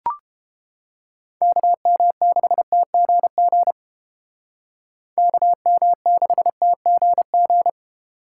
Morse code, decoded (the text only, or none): KM6TGG KM6TGG